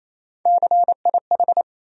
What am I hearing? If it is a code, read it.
Morse code, heard as a keyed tone, about 28 wpm, 700 Hz. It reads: CIH
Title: USA Call Sign Postfixes - 3 Letters 28wpm